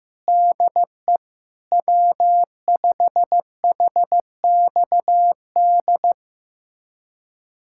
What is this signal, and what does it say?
Morse code, heard as a keyed tone, about 15 wpm, 700 Hz. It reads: DE W5HXD
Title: USA Callsign DE Callsign 15wpm